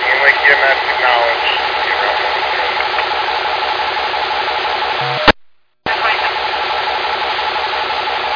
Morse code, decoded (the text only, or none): none